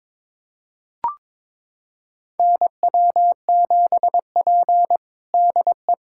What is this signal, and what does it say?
Morse code, heard as a keyed tone, about 22 wpm, 700 Hz. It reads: NW7P DE
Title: USA Callsign DE Callsign - 4 Characters - 22wpm